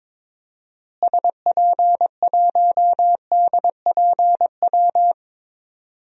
Morse code, heard as a keyed tone, about 22 wpm, 700 Hz. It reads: SP1DPW
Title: DXCC-100 Call Signs - 6 Characters - Repeated 2x - 22wpm